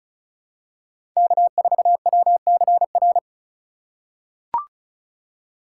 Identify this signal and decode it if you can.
Morse code, heard as a keyed tone, about 34 wpm, 700 Hz. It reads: K4WCR
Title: USA Call Signs - 5 Characters - 4x Character Spacing - 35wpm